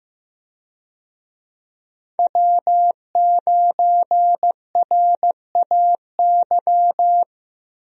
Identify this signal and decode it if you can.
Morse code, heard as a keyed tone, about 15 wpm, 700 Hz. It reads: W9RAY
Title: USA Call Signs - 5 Characters - Repeated 2x - 15wpm